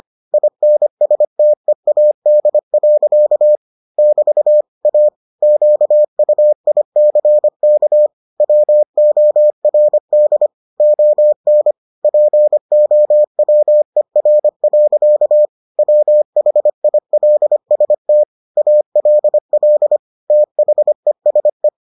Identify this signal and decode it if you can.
Morse code, heard as a keyed tone, about 25 wpm, 600 Hz. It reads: INSTEAD. <BT> A QUICK WORD ON POWER. WHILST ALL THESE